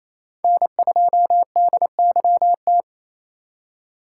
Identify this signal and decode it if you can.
Morse code, heard as a keyed tone, about 28 wpm, 700 Hz. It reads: N2DYT
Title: K1USN SST QSO Practice - 28wpm